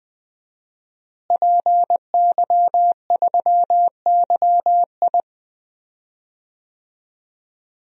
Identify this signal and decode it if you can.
Morse code, heard as a keyed tone, about 20 wpm, 700 Hz. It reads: PY3YI
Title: DXCC-100 Call Signs - 5 Characters - Repeated 3x - 20wpm